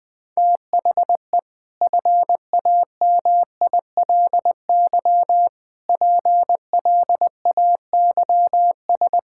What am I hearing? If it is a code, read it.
Morse code, heard as a keyed tone, about 20 wpm, 700 Hz. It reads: THE FAMILY PLAYS
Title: Easy Sentences with Hints - 20wpm